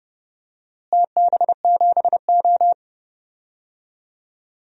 Morse code, heard as a keyed tone, about 31 wpm, 700 Hz. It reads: TB7O
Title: DXCC-100 Call Signs - 4 Characters - Repeated 2x - 30wpm